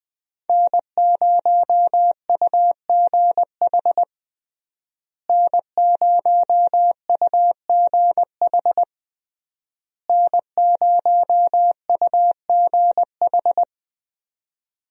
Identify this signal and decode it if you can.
Morse code, heard as a keyed tone, about 21 wpm, 700 Hz. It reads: N0UGH N0UGH N0UGH
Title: USA Call Signs - 5 Characters - Repeated 3x - 20wpm